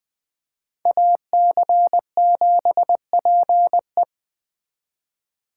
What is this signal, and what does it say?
Morse code, heard as a keyed tone, about 20 wpm, 700 Hz. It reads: AC7PE